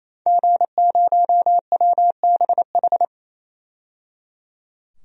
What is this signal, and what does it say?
Morse code, heard as a keyed tone, about 28 wpm, 700 Hz. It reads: G0WBH
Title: DXCC-100 Call Signs - 5 Characters - 28wpm